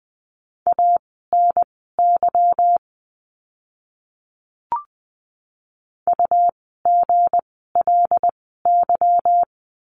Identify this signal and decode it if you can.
Morse code, heard as a keyed tone, about 20 wpm, spaced as for 15 wpm, 700 Hz. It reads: ANY UGLY
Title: 160 Lesson 26f - Letter g - Words 20wpm